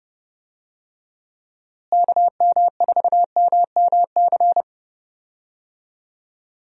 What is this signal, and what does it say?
Morse code, heard as a keyed tone, about 30 wpm, 700 Hz. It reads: KM4MMC
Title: USA Call Signs - 6 Characters - Repeated 2x - 30wpm